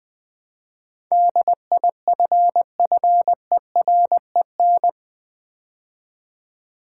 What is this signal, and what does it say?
Morse code, heard as a keyed tone, about 20 wpm, 700 Hz. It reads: DIFFEREN
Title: Build A Word - 4 Letters At A Time - 10-Letter Words - 20wpm